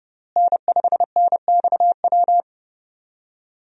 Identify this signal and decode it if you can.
Morse code, heard as a keyed tone, about 30 wpm, 700 Hz. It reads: N5NXW
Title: USA Call Signs - 5 Characters - Repeated 2x - 30wpm